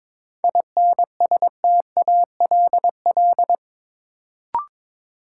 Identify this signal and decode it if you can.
Morse code, heard as a keyed tone, about 22 wpm, 700 Hz. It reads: INSTALL
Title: Sentences from Top 1600 Words - New Word Drill - 22wpm